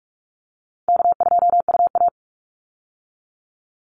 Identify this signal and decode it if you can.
Morse code, heard as a keyed tone, about 44 wpm, 700 Hz. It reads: K2UA
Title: Easy Call Signs - 45wpm